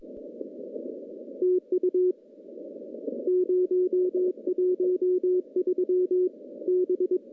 Morse code, heard as a keyed tone, about 22 wpm, 360 Hz. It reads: TU 013 B
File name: contests / CW-SS-k5tr- -0200